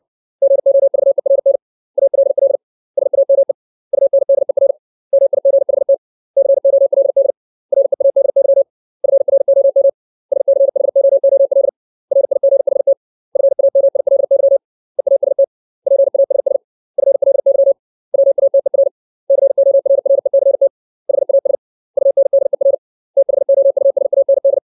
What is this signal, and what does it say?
Morse code, heard as a keyed tone, about 60 wpm, 550 Hz. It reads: GOVERN ALL SAME UNDER MIGHT COLD READY FROM SCHOOL RIGHT FAMILY EAST PASS FLY WATER CORRECT HAS UNDER THOUSAND
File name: 632 words / 632 Words 60WPM